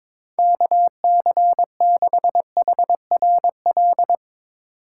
Morse code, heard as a keyed tone, about 22 wpm, 700 Hz. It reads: KC6HRL
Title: USA Call Signs - 2x Character Spacing - 22wpm